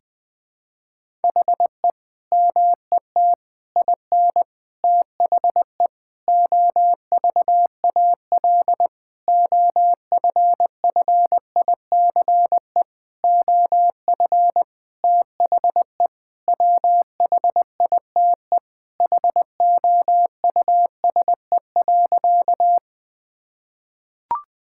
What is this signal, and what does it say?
Morse code, heard as a keyed tone, about 20 wpm, 700 Hz. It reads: HE MET IN THE OVAL OFFICE OF THE WHITE HOUSE.
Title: Sentences with 8-Word Phrases - Spoken Beginning - 20wpm